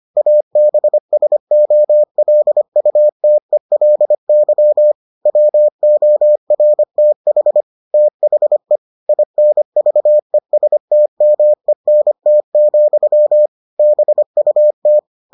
Morse code, heard as a keyed tone, about 25 wpm, 600 Hz. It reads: ABSOLUTELY WORTH THE INVESTMENT, BUT